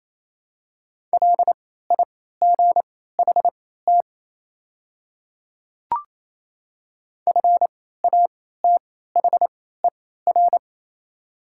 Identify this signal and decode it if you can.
Morse code, heard as a keyed tone, about 28 wpm, spaced as for 16 wpm, 700 Hz. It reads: LIGHT FATHER
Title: 3x Character Spacing - Top 200 Words - 28wpm